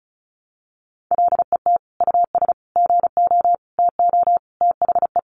Morse code, heard as a keyed tone, about 34 wpm, 700 Hz. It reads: LET US GO TO THE
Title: Sentences from Top 1400 Words 35wpm